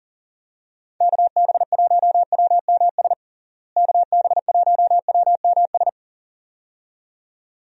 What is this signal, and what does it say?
Morse code, heard as a keyed tone, about 37 wpm, 700 Hz. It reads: KB1WMS KB1WMS
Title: USA Call Signs - 6 Characters - Repeated 2x - 40wpm